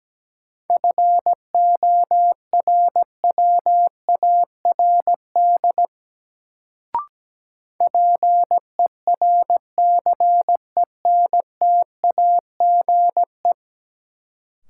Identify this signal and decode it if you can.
Morse code, heard as a keyed tone, about 17 wpm, 700 Hz. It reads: FORWARD PERCENTAGE
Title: Sentences from Top 1300 Words - New Word Drill - 17wpm